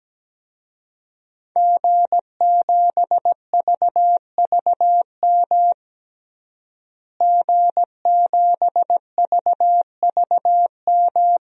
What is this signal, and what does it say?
Morse code, heard as a keyed tone, about 17 wpm, 700 Hz. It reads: G7VVM G7VVM
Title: DXCC-100 Call Signs - 5 Characters - Repeated 3x - 17wpm